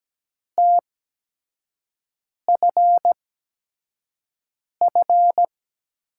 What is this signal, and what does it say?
Morse code, heard as a keyed tone, about 17 wpm, spaced as for 5 wpm, 700 Hz. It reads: TFF
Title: USA Call Signs - 5 Characters - 8x Character Spacing - 17wpm